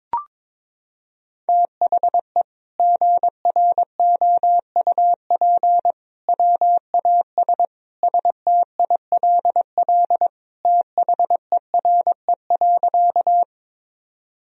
Morse code, heard as a keyed tone, about 22 wpm, 700 Hz. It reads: THE GROUP WAS STILL THERE.
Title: Sentences from Top 1000 Words - Review Part 1 - 22wpm